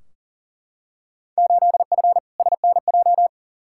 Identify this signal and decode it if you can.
Morse code, heard as a keyed tone, about 39 wpm, 700 Hz. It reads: 8F SNJ
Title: Field Day Exchanges Encore - 40wpm